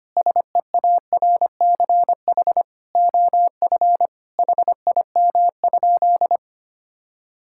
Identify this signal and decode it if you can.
Morse code, heard as a keyed tone, about 25 wpm, 700 Hz. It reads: SEARCH OF HIM?
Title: Sentences from Top 1000 Words - Review Part 5 - 25wpm